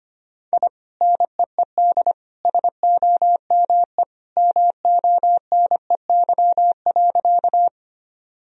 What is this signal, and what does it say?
Morse code, heard as a keyed tone, about 25 wpm, 700 Hz. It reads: I NEED SOME MONEY.
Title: Sentences from Top 1000 Words - Review Part 1 - 25wpm